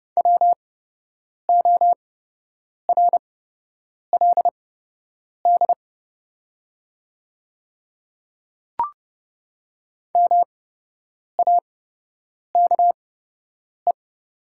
Morse code, heard as a keyed tone, about 31 wpm, spaced as for 8 wpm, 700 Hz. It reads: WORLD MAKE
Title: 8x Character Spacing - Top 500 Words - 30wpm